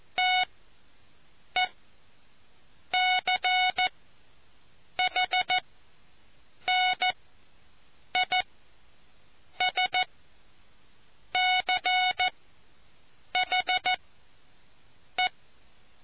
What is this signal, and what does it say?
Morse code, heard as a keyed tone, about 14 wpm, spaced as for 6 wpm, 740 Hz. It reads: TECHNISCHE